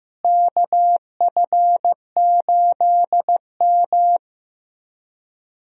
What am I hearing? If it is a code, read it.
Morse code, heard as a keyed tone, about 15 wpm, 700 Hz. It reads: KF8M
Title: USA Call Signs - 4 Characters - Repeated 2x - 15wpm